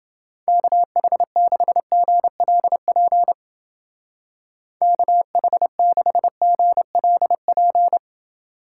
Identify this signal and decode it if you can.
Morse code, heard as a keyed tone, about 29 wpm, 700 Hz. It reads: KH6GLP KH6GLP